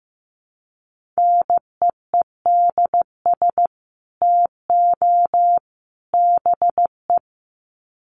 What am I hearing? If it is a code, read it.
Morse code, heard as a keyed tone, about 15 wpm, 700 Hz. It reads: NEEDS TO BE
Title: Sets of 3 Words 15wpm